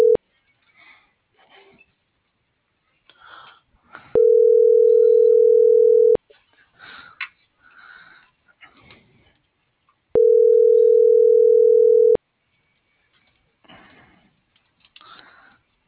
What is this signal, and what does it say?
Ringback tone (North America), 440 + 480 Hz.